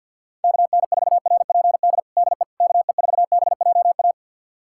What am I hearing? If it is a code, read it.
Morse code, heard as a keyed tone, about 52 wpm, 700 Hz. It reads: KN4RPD DE KE4BJA
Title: USA Callsign DE Callsign - 6 Characters - 50wpm